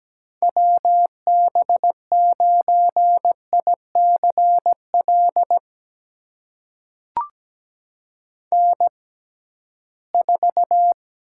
Morse code, heard as a keyed tone, about 17 wpm, spaced as for 13 wpm, 700 Hz. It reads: WB9ICL N4
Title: USA Call Signs - 6x Character Spacing - 17wpm